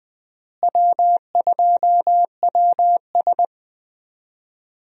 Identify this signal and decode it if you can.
Morse code, heard as a keyed tone, about 20 wpm, 700 Hz. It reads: W2WS
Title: USA Call Signs - 4 Characters - Repeated 3x - 20wpm